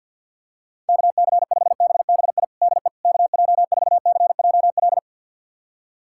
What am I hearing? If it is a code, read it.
Morse code, heard as a keyed tone, about 52 wpm, 690 Hz. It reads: KC5BBI DE KJ4CJL
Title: USA Callsign DE Callsign - 6 Characters - 50wpm